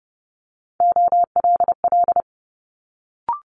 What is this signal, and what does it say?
Morse code, heard as a keyed tone, about 32 wpm, 700 Hz. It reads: OLL
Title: Multi-letter Phonemes Quick Fire 30wpm